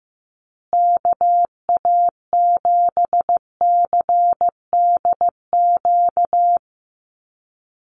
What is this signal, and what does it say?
Morse code, heard as a keyed tone, about 15 wpm, 700 Hz. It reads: KA7CDQ
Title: USA Call Signs Repeated 2x 15wpm